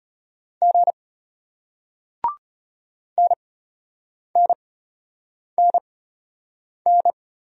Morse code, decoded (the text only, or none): G NNNN